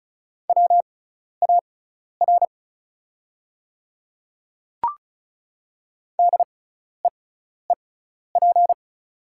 Morse code, heard as a keyed tone, about 36 wpm, spaced as for 12 wpm, 700 Hz. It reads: WAR DEEP
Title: 6x Character Spacing - Top 500 Words - 35wpm